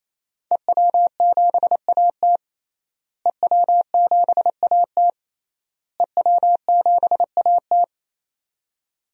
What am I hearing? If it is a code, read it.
Morse code, heard as a keyed tone, about 28 wpm, 700 Hz. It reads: EW7AT EW7AT EW7AT
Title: DXCC-100 Call Signs - 5 Characters - Repeated 3x - 28wpm